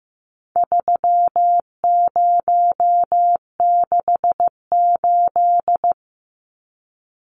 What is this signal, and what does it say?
Morse code, heard as a keyed tone, about 15 wpm, 700 Hz. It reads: 3068
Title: Numbers - 4-Digit 15wpm